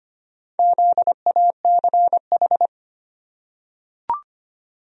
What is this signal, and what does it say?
Morse code, heard as a keyed tone, about 25 wpm, 700 Hz. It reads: ZACH
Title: USA Common Names - Max 4 Letters - 25wpm